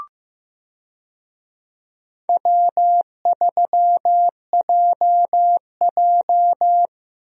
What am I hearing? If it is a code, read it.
Morse code, heard as a keyed tone, about 15 wpm, 700 Hz. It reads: W3JJ
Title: USA Call Signs - 4 Characters - Repeated 3x - 15wpm